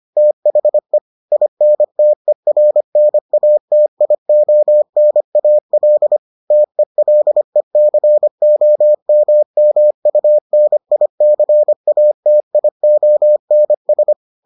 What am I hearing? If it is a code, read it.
Morse code, heard as a keyed tone, about 25 wpm, 600 Hz. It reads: THE INTERNATIONAL TELECOMMUNICATIONS